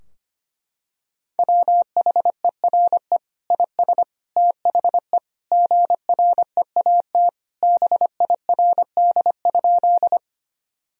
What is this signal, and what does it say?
Morse code, heard as a keyed tone, about 25 wpm, 700 Hz. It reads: WHERE IS THE GREAT BIRD?
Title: Sentences from Top 1000 Words - Review Part 2 - 25wpm